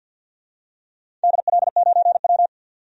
Morse code, heard as a keyed tone, about 51 wpm, 700 Hz. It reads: DL9W
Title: DXCC-100 Call Signs - 4 Characters - 50wpm